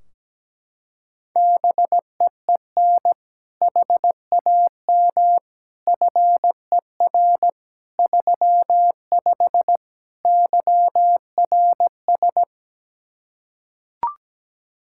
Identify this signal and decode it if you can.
Morse code, heard as a keyed tone, about 17 wpm, 700 Hz. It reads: BEEN HAM FER 35 YRS
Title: Practice QSO - Exchange 3 - More Info - 17wpm